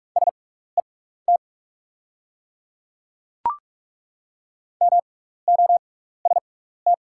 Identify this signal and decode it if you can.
Morse code, heard as a keyed tone, about 48 wpm, spaced as for 16 wpm, 700 Hz. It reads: SET MOST
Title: 6x Character Spacing - Top 300 Words - 45wpm